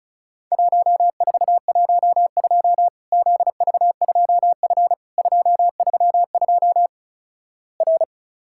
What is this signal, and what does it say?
Morse code, heard as a keyed tone, about 36 wpm, 700 Hz. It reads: 1412 ZV2F 232 R
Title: Worked All Europe DX Contest - QTC Exchange - 35wpm